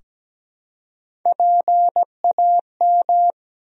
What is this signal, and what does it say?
Morse code, heard as a keyed tone, about 17 wpm, 700 Hz. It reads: PAM
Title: USA Common Names - Max 3 Letters - 17wpm